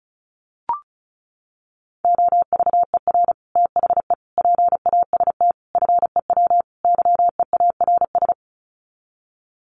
Morse code, heard as a keyed tone, about 34 wpm, 700 Hz. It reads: OVER THE PAST FEW YEARS
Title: Sets of 5 Words Encore 35wpm